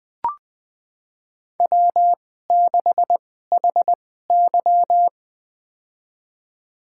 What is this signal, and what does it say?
Morse code, heard as a keyed tone, about 20 wpm, spaced as for 15 wpm, 700 Hz. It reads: W6HY
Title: USA Call Signs - 4 Characters - 2x Character Spacing - 20wpm